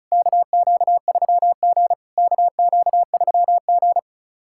Morse code, heard as a keyed tone, about 36 wpm, 690 Hz. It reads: KQ3G KQ3G
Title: SOTA QSO Practice 35wpm